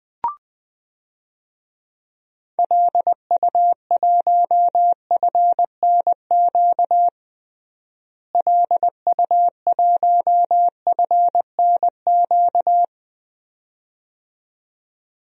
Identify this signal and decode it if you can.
Morse code, heard as a keyed tone, about 20 wpm, 700 Hz. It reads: LU1FNQ LU1FNQ
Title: DXCC-100 Call Signs - 6 Characters - Repeated 2x - 20wpm